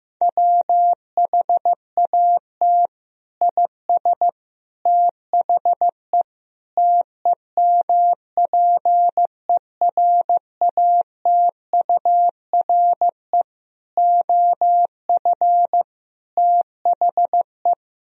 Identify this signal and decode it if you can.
Morse code, heard as a keyed tone, about 16 wpm, 700 Hz. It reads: WHAT IS THE TEMPERATURE OF THE
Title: Sentences from Top 1000 Words - Review Part 4 - 15wpm